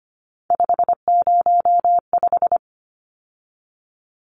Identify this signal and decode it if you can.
Morse code, heard as a keyed tone, about 25 wpm, 700 Hz. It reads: 505
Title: Numbers - 3-Digit 25wpm